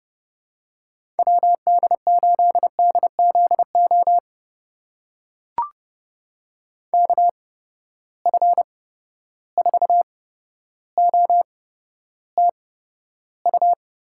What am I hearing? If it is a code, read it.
Morse code, heard as a keyed tone, about 30 wpm, spaced as for 13 wpm, 700 Hz. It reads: WD8DZO KF4OTU
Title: USA Call Signs - 6 Characters - 8x Character Spacing - 30wpm